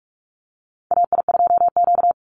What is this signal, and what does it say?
Morse code, heard as a keyed tone, about 44 wpm, 700 Hz. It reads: AI2Q
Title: Call Signs Repeated 2x - 45wpm